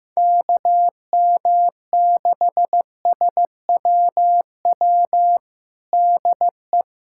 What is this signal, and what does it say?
Morse code, heard as a keyed tone, about 15 wpm, 700 Hz. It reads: KM6SWW DE
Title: USA Callsign DE Callsign 15wpm